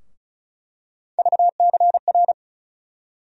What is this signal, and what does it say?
Morse code, heard as a keyed tone, about 36 wpm, 700 Hz. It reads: VCR
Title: USA Call Sign Postfixes - 3 Letters 35wpm